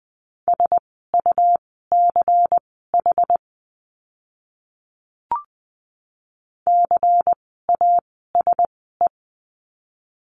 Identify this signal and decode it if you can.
Morse code, heard as a keyed tone, about 20 wpm, spaced as for 15 wpm, 700 Hz. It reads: SUCH CASE
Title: 115 Lesson 19f - Letter w - Words 20wpm